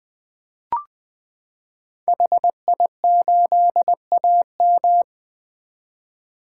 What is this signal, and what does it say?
Morse code, heard as a keyed tone, about 21 wpm, 700 Hz. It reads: HI8AM
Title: 2023 POTA Activator Call Signs - 20wpm